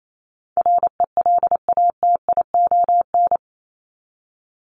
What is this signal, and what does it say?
Morse code, heard as a keyed tone, about 28 wpm, 700 Hz. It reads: RELATION
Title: ES:S Suffix - Top 100 Words with Base Words 28wpm